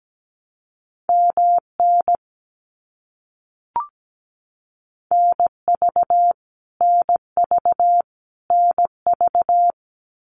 Morse code, heard as a keyed tone, about 17 wpm, 700 Hz. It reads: MN NV NV NV
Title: US States Repeated 3x 17wpm